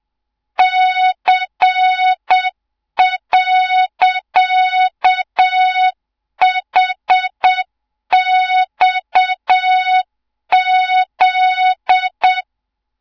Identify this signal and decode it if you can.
Morse code, heard as a keyed tone, about 7 wpm, 3700 Hz, with noise. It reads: C.HXZ
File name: Morze-HA2MN / hang / 24 GYAKORLAT